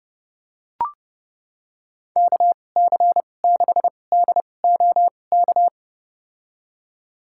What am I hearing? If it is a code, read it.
Morse code, heard as a keyed tone, about 29 wpm, spaced as for 21 wpm, 700 Hz. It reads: KC6DOK
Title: USA Call Signs - 6 Characters - 2x Character Spacing - 30wpm